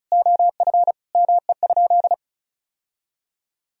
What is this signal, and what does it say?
Morse code, heard as a keyed tone, about 36 wpm, 700 Hz. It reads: OF ME?
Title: Sentences from Top 1000 Words - Review Part 4 - 35wpm